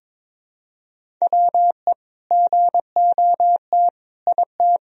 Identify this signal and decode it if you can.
Morse code, heard as a keyed tone, about 22 wpm, 700 Hz. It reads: WE GOT IT
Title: Sentences from Top 1000 Words - Review Part 1 - 22wpm